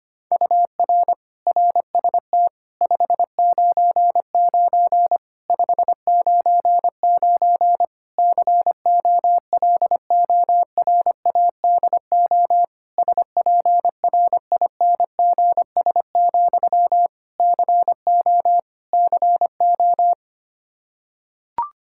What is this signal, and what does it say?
Morse code, heard as a keyed tone, about 25 wpm, 700 Hz. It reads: UR RST 599 599 COLORADO SPRINGS, CO CO
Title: K1USN SST QSO Practice 25wpm